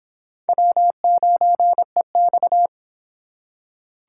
Morse code, heard as a keyed tone, about 25 wpm, 700 Hz. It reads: W9EX